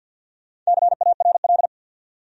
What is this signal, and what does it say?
Morse code, heard as a keyed tone, about 48 wpm, 700 Hz. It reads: CARL
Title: USA Common Names - 50wpm